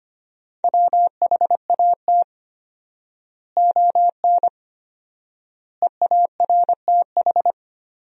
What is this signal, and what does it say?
Morse code, heard as a keyed tone, about 25 wpm, 700 Hz. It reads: WHAT ON EARTH
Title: 4x Word-Spacing - Sentences from Top 200 Words - 25wpm